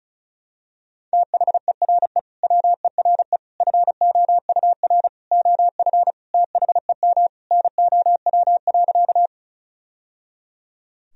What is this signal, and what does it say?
Morse code, heard as a keyed tone, about 36 wpm, 700 Hz. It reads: THERE WERE FOUR OF THEM NOW.
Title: Sentences from Top 1000 Words - Review Part 1 - 35wpm